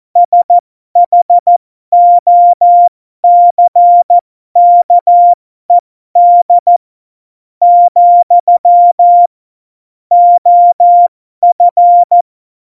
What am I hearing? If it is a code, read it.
Morse code, heard as a keyed tone, about 14 wpm, spaced as for 12 wpm, 700 Hz. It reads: SHOCKED , OF
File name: Morsekurs / Witze